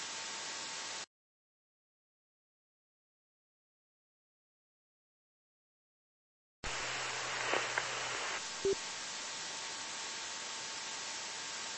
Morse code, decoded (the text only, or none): none